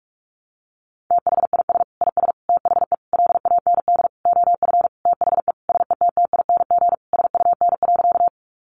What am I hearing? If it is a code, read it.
Morse code, heard as a keyed tone, about 44 wpm, 700 Hz. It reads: THIS IS THE LAND OF THE SETTING SUN.
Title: Sentences from Top 1300 Words 45wpm